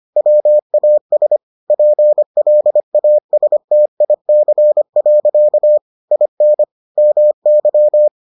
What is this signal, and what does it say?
Morse code, heard as a keyed tone, about 25 wpm, 600 Hz. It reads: WAS PLASTIC. IN MY